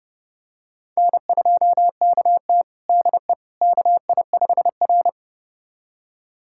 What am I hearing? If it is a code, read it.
Morse code, heard as a keyed tone, about 30 wpm, 700 Hz. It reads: N2KT DE KI5R
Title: USA Callsign DE Callsign - 4 Characters - 30wpm